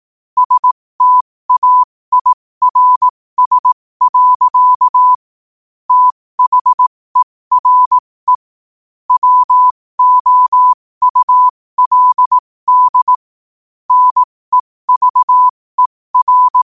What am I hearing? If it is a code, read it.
Morse code, heard as a keyed tone, about 18 wpm, spaced as for 15 wpm, 1000 Hz. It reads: STAIRS. THERE WOULD NEVER
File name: qotdfiles / 15wpm